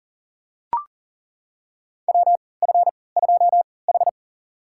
Morse code, heard as a keyed tone, about 40 wpm, spaced as for 23 wpm, 700 Hz. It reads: WF2H
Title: USA Call Signs - 4 Characters - 3x Character Spacing - 40wpm